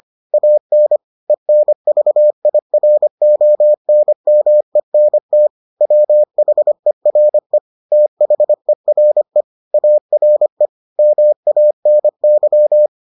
Morse code, heard as a keyed tone, about 25 wpm, 600 Hz. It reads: AN ENVIRONMENT WHERE THERE ARE MANY